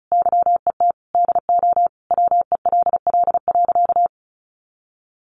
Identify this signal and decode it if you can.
Morse code, heard as a keyed tone, about 34 wpm, 700 Hz. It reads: YET DO WELL.
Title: Sentences from Top 1400 Words 35wpm